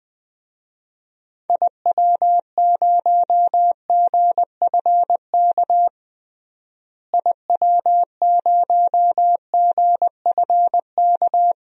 Morse code, heard as a keyed tone, about 21 wpm, 700 Hz. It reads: IW0GFK IW0GFK
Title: DXCC-100 Call Signs - 6 Characters - Repeated 2x - 20wpm